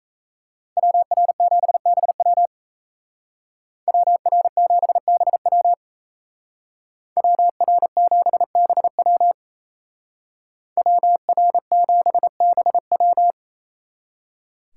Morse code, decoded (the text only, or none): WR7BW WR7BW WR7BW WR7BW